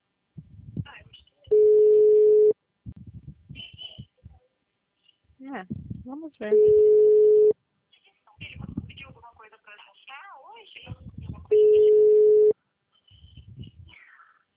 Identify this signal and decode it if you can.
Ringback tone (Europe, 425 Hz), with speech.